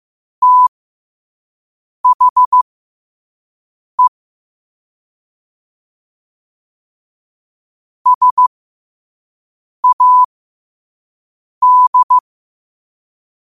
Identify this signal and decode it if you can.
Morse code, heard as a keyed tone, about 15 wpm, spaced as for 5 wpm, 1000 Hz. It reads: THE SAD
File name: qotdfiles / 5wpm